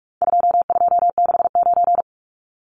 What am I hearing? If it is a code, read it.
Morse code, heard as a keyed tone, about 44 wpm, 700 Hz. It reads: 2269
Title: Numbers - 4-Digit 45wpm